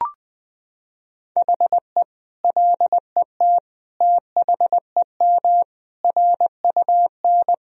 Morse code, heard as a keyed tone, about 20 wpm, 700 Hz. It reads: HE LET THEM RUN
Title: Sentences from Top 1000 Words - Review Part 3 - 20wpm